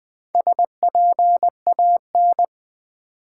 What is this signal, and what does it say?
Morse code, heard as a keyed tone, about 21 wpm, 700 Hz. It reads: SPAN
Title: Build A Word - 2 Letters At A Time - 4-Letter Words - 20wpm